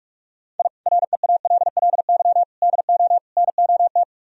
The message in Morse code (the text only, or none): I REALLY DO NOT